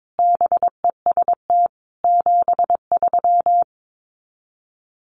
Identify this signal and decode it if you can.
Morse code, heard as a keyed tone, about 22 wpm, 700 Hz. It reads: BEST 73
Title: Common QSO Phrases 22wpm